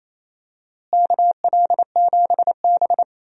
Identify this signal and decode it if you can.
Morse code, heard as a keyed tone, about 28 wpm, 700 Hz. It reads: KL7B
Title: DXCC-100 Call Signs - 4 Characters - Repeated 2x - 28wpm